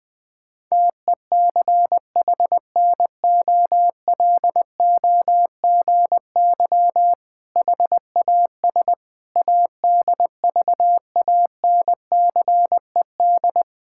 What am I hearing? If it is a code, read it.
Morse code, heard as a keyed tone, about 20 wpm, 700 Hz. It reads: TECHNOLOGY HAS ADVANCED
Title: Sentences with 6-Word Phrases - Spoken Beginning - 20wpm